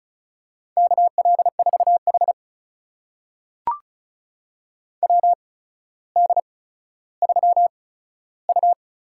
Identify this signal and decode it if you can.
Morse code, heard as a keyed tone, about 36 wpm, spaced as for 15 wpm, 700 Hz. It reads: KL4H WD3U
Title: USA Call Signs - 4 Characters - 8x Character Spacing - 35wpm